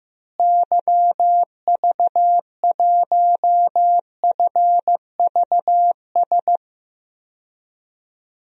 Morse code, heard as a keyed tone, about 15 wpm, 700 Hz. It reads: YV1FVS
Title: DXCC-100 Call Signs - 6 Characters - Repeated 2x - 15wpm